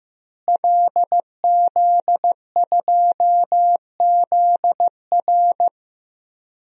Morse code, heard as a keyed tone, about 15 wpm, 700 Hz. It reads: LZ2ZR
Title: DXCC-100 Call Signs - 5 Characters - Repeated 3x - 15wpm